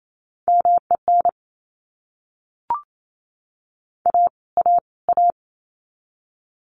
Morse code, heard as a keyed tone, about 28 wpm, 700 Hz. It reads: MEN A A A
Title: Top 500 Words Repeated 28wpm